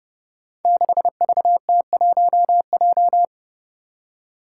Morse code, heard as a keyed tone, about 30 wpm, 700 Hz. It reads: 6VT1J
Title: Five Challenging Letter-Number Combinations 30wpm